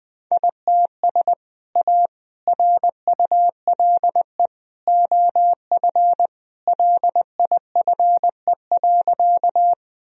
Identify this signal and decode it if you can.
Morse code, heard as a keyed tone, about 19 wpm, 700 Hz. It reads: ITS A RULE OF LIFE.
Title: Sentences from Top 1000 Words - Review Part 2 - 20wpm